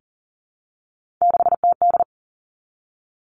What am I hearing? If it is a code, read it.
Morse code, heard as a keyed tone, about 42 wpm, 700 Hz. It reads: 6TD
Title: Three Challenging Letter-Number Combinations 40wpm